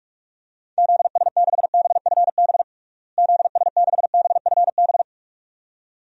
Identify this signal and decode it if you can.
Morse code, heard as a keyed tone, about 46 wpm, 700 Hz. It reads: ZS6BFB ZS6BFB
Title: DXCC-100 Call Signs - 6 Characters - Repeated 2x - 45wpm